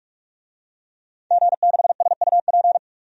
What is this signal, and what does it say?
Morse code, heard as a keyed tone, about 46 wpm, 700 Hz. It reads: G6SUP
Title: DXCC-100 Call Signs - 5 Characters - 45wpm